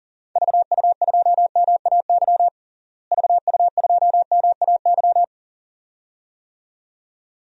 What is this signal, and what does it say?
Morse code, heard as a keyed tone, about 37 wpm, 700 Hz. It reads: VU2MAY VU2MAY
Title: DXCC-100 Call Signs - 6 Characters - Repeated 2x - 40wpm